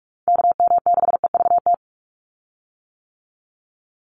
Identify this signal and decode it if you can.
Morse code, heard as a keyed tone, about 44 wpm, 700 Hz. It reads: KM6EVT
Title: USA Call Signs 45wpm